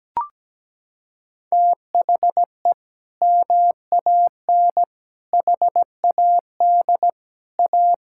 Morse code, heard as a keyed tone, about 17 wpm, 700 Hz. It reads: THE MAN HAD A
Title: Sentences from Top 1000 Words - Review Part 6 - 17wpm